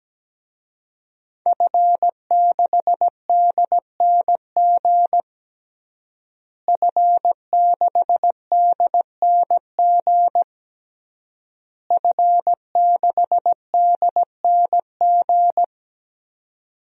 Morse code, heard as a keyed tone, about 17 wpm, 700 Hz. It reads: F6DNG F6DNG F6DNG
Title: DXCC-100 Call Signs - 5 Characters - Repeated 3x - 17wpm